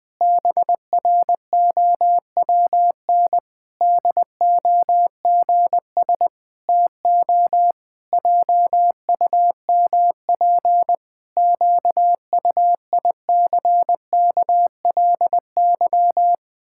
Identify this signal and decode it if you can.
Morse code, heard as a keyed tone, about 20 wpm, 700 Hz. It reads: BROWN DOGS TO JUMP QUICKLY
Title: Pangram Challenge v2 - 20wpm